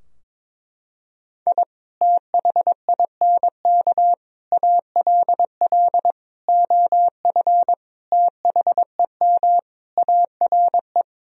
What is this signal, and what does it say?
Morse code, heard as a keyed tone, about 22 wpm, 700 Hz. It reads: I THINK ALL OF THEM ARE
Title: Sentences from Top 1000 Words - Review Part 4 - 22wpm